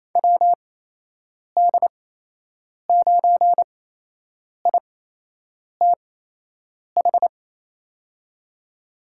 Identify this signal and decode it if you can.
Morse code, heard as a keyed tone, about 28 wpm, spaced as for 8 wpm, 700 Hz. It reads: WD9ITH
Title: USA Call Signs - 8x Character Spacing - 28wpm